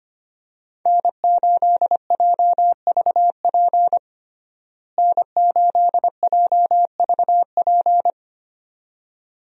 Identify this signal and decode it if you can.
Morse code, heard as a keyed tone, about 25 wpm, 700 Hz. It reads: N8JVP N8JVP
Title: USA Call Signs - 5 Characters - Repeated 2x - 25wpm